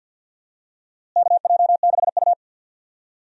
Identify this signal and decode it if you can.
Morse code, heard as a keyed tone, about 49 wpm, 700 Hz. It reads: KJ6U
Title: USA Call Signs - 4 Characters - Repeated 2x - 50wpm